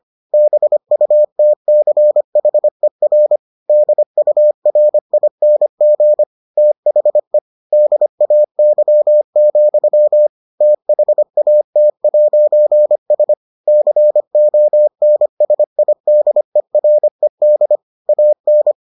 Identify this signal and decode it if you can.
Morse code, heard as a keyed tone, about 25 wpm, 600 Hz. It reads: BUTCHER DURING THE DAY, THAT'S CONSIDERED AN